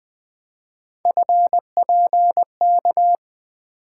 Morse code, heard as a keyed tone, about 20 wpm, 700 Hz. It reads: FPK
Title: Three Letter-Number Combinations - 20wpm